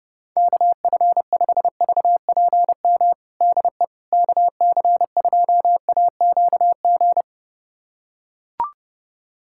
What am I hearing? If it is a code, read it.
Morse code, heard as a keyed tone, about 30 wpm, 690 Hz. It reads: KF5VPM DE KC2AQG
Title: USA Callsign DE Callsign - 6 Characters - 30wpm